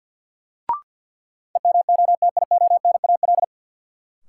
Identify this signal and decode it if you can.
Morse code, heard as a keyed tone, about 51 wpm, 700 Hz. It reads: EMOTIONAL
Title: Sentences from Top 1000 Words - Review Part 5 - New Word Drill - 50wpm